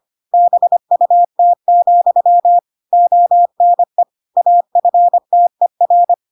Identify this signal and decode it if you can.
Morse code, heard as a keyed tone, about 25 wpm, 700 Hz. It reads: BUT, ONE AFTER
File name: Pinocchio / Pinocchio ch7 25 25 ik1hge